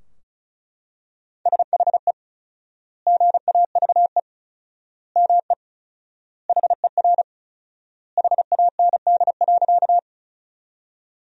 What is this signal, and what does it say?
Morse code, heard as a keyed tone, about 36 wpm, 700 Hz. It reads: SHE GAVE ME HER HAND.
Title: 4x Word-Spacing - Sentences from Top 500 Words - 35wpm